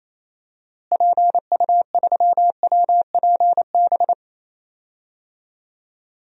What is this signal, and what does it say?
Morse code, heard as a keyed tone, about 27 wpm, 700 Hz. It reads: PU3WPB
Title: DXCC-100 Call Signs - 6 Characters - Repeated 3x - 28wpm